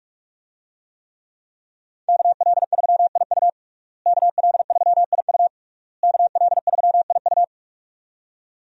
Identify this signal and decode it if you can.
Morse code, heard as a keyed tone, about 48 wpm, 700 Hz. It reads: KL3IU KL3IU KL3IU
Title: USA Call Signs - 5 Characters - Repeated 3x - 45wpm